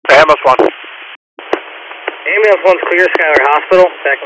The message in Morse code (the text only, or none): none